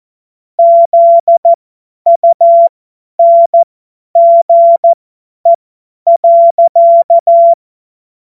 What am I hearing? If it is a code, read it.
Morse code, heard as a keyed tone, about 14 wpm, spaced as for 10 wpm, 680 Hz. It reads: ZUNGE.
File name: Morsekurs / Witze / Witz 25. Oktober 2022 50 F70